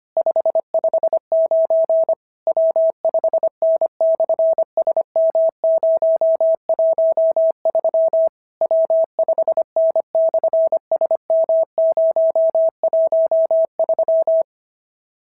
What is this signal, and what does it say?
Morse code, heard as a keyed tone, about 25 wpm, 650 Hz. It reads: 559 W5N/SM013 W5N/SM013
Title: SOTA QSO Practice 25wpm